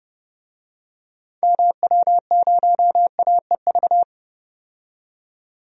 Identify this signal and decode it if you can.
Morse code, heard as a keyed tone, about 31 wpm, 700 Hz. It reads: MW0AEV
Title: DXCC-100 Call Signs - 6 Characters - 30wpm